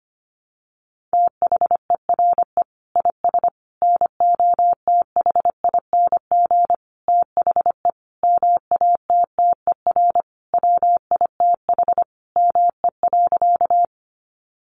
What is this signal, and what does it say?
Morse code, heard as a keyed tone, about 25 wpm, 700 Hz. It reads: THERE IS NOTHING THE MATTER WITH ME.
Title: Sentences from Top 700 Words 25wpm